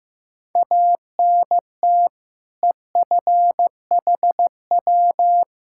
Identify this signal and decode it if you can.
Morse code, heard as a keyed tone, about 15 wpm, 700 Hz. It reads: ANT EFHW
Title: Practice QSO - Exchange 2 - Info - 15wpm